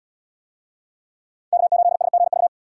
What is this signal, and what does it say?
Morse code, heard as a keyed tone, about 72 wpm, 700 Hz. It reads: K8SGV